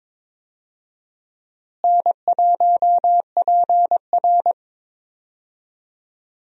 Morse code, heard as a keyed tone, about 22 wpm, 700 Hz. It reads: N1PR
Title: USA Call Signs - 4 Characters - Repeated 2x - 22wpm